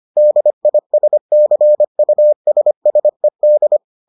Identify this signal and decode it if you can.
Morse code, heard as a keyed tone, about 25 wpm, 600 Hz. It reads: DISCUSSED